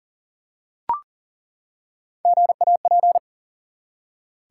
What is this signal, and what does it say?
Morse code, heard as a keyed tone, about 40 wpm, 700 Hz. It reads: GAP
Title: Three Letter Words - 40wpm